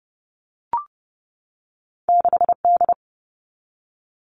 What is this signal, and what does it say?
Morse code, heard as a keyed tone, about 30 wpm, 700 Hz. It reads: 6D
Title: All Cut Numbers - 2-Digit 30wpm